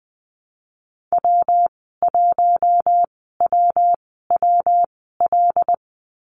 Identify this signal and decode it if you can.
Morse code, heard as a keyed tone, about 19 wpm, spaced as for 14 wpm, 700 Hz. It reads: W1WWL
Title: 167 Lesson 27g - Number 7 - Call Signs 20wpm